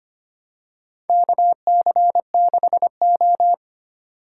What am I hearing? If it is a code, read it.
Morse code, heard as a keyed tone, about 25 wpm, 700 Hz. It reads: KC6O
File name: practice / 25wpm / Four Letter-Number Combinations - 25wpm